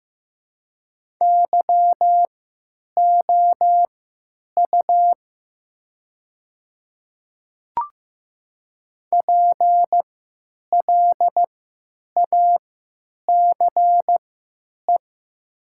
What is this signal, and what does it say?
Morse code, heard as a keyed tone, about 15 wpm, spaced as for 9 wpm, 700 Hz. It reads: YOU PLACE
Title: 3x Character Spacing - Top 500 Words - 15wpm